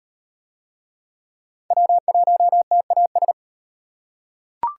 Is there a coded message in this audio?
Yes: Morse code from 1.7 to 3.3 s, heard as a keyed tone, about 34 wpm, 700 Hz. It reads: W1TAS